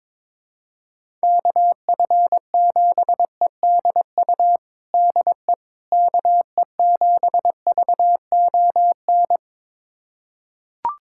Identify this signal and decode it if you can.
Morse code, heard as a keyed tone, about 22 wpm, 700 Hz. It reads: KF7EDU DE KE7VON
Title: USA Callsign DE Callsign - 6 Characters - 22wpm